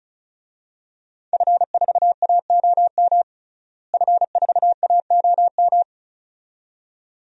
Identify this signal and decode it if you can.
Morse code, heard as a keyed tone, about 36 wpm, 700 Hz. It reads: F4AOM F4AOM
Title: DXCC-100 Call Signs - 5 Characters - Repeated 2x - 35wpm